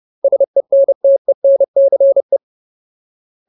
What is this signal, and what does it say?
Morse code, heard as a keyed tone, about 30 wpm, 550 Hz. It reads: SENTENCE
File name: Ultimate Word List / 30WPM Ultimate